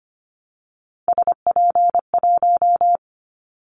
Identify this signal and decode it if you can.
Morse code, heard as a keyed tone, about 25 wpm, 700 Hz. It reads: SP1
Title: Callsign Prefixes for Top 25 DXCC - Repeated 3x - 25wpm